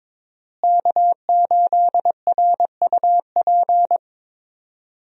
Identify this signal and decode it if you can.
Morse code, heard as a keyed tone, about 22 wpm, 700 Hz. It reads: K8RUP
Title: USA Call Signs - 5 Characters - Repeated 3x - 22wpm